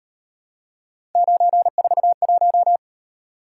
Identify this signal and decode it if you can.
Morse code, heard as a keyed tone, about 38 wpm, 700 Hz. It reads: 941